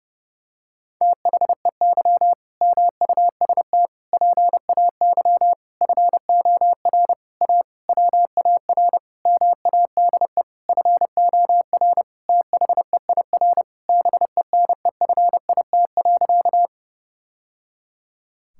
Morse code, heard as a keyed tone, about 30 wpm, 700 Hz. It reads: THEY MUST PAY FOR A WAR MADE FOR THEIR BENEFIT.
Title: Sentences from Top 1000 Words - Review Part 6 - 30wpm